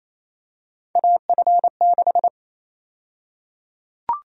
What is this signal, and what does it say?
Morse code, heard as a keyed tone, about 28 wpm, 700 Hz. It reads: AF6
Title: USA Call Sign Prefixes - Repeated 2x - 28wpm